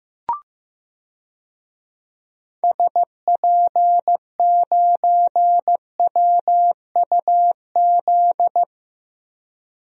Morse code, heard as a keyed tone, about 15 wpm, 700 Hz. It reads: SP9WUZ
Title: DXCC-100 Call Signs - 6 Characters - 15wpm